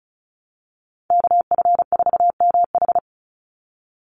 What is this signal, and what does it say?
Morse code, heard as a keyed tone, about 36 wpm, 700 Hz. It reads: KF4MH
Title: NA SOTA Chaser Callsigns 35wpm